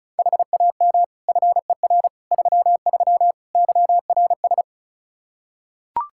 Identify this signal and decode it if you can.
Morse code, heard as a keyed tone, about 36 wpm, 700 Hz. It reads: HAM FER 33 YRS
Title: Practice QSO - Exchange 3 - More Info - 35wpm